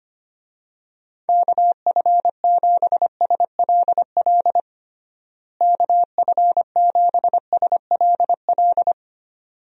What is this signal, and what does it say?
Morse code, heard as a keyed tone, about 25 wpm, 700 Hz. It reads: KF7SLL KF7SLL
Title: USA Call Signs - 6 Characters - Repeated 2x - 25wpm